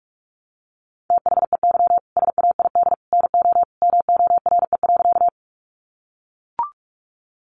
Morse code, heard as a keyed tone, about 44 wpm, 700 Hz. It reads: THEY SAID NO MORE.
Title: Sentences from Top 100 Words 45wpm